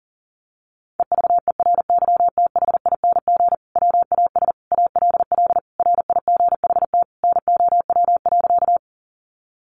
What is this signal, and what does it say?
Morse code, heard as a keyed tone, about 40 wpm, 700 Hz. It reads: EVERYTHING WAS ALL RIGHT NOW.
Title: Sentences from Top 700 Words 40wpm